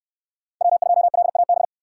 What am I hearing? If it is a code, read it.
Morse code, heard as a keyed tone, about 67 wpm, 700 Hz. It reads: W2LAB